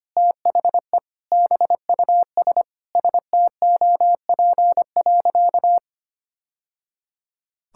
Morse code, heard as a keyed tone, about 25 wpm, 700 Hz. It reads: THE BUS STOP.
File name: practice / 25wpm / Sentences with 2-Word Phrases - 25wpm